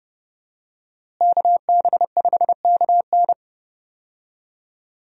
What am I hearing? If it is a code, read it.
Morse code, heard as a keyed tone, about 30 wpm, 700 Hz. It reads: KB5KN
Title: USA Call Signs - 5 Characters - Repeated 2x - 30wpm